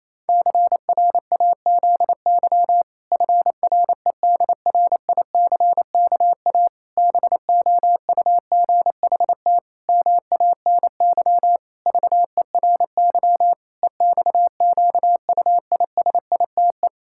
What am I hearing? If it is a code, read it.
Morse code, heard as a keyed tone, about 28 wpm, 700 Hz. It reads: CRAZY FREDRICKA BOUGHT MANY VERY EXQUISITE
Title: Pangram Challenge v2 - 28wpm